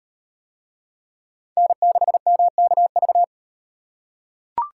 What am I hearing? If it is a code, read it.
Morse code, heard as a keyed tone, about 34 wpm, 700 Hz. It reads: N6MKV